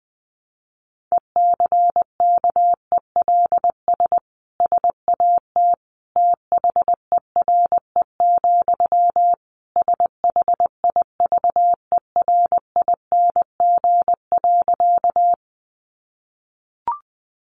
Morse code, heard as a keyed tone, about 20 wpm, 700 Hz. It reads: ECKELS SAT THERE, SHIVERING.
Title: A Sound Of Thunder 20wpm